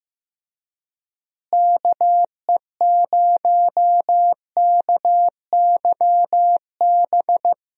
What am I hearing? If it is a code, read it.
Morse code, heard as a keyed tone, about 15 wpm, 700 Hz. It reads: KE0KYB
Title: USA Call Signs - 6 Characters - Repeated 3x - 15wpm